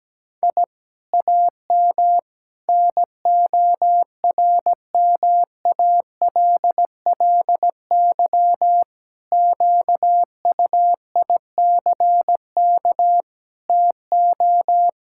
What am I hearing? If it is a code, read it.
Morse code, heard as a keyed tone, about 17 wpm, 700 Hz. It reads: I AM NORMALLY QUICK TO
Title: Sentences from Top 1000 Words - Review Part 6 - 17wpm